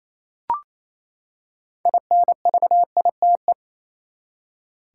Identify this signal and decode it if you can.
Morse code, heard as a keyed tone, about 28 wpm, 700 Hz. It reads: INVITE
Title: Six Letter Words - 28wpm